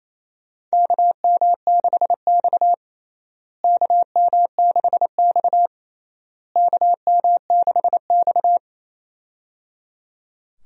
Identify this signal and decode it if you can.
Morse code, heard as a keyed tone, about 28 wpm, 700 Hz. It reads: KM6X KM6X KM6X
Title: USA Call Signs - 4 Characters - Repeated 3x - 28wpm